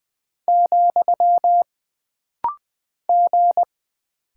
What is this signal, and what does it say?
Morse code, heard as a keyed tone, about 20 wpm, 700 Hz. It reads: , G
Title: Single Character (Letter:Number:Punctuation:Prosign) - Rapid-Fire - 20wpm